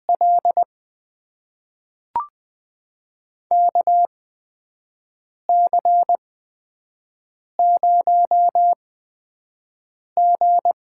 Morse code, heard as a keyed tone, about 20 wpm, spaced as for 5 wpm, 700 Hz. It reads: L KC0G
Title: USA Call Signs - 6 Characters - 8x Character Spacing - 20wpm